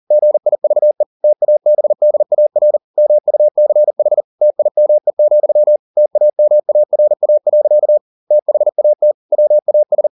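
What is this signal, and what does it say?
Morse code, heard as a keyed tone, about 40 wpm, 600 Hz. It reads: GIVE TABDAR MUCH TIME, TAMARA. THAT WAS
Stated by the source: Morse code at 40 wpm, practice text, CW med Gnister.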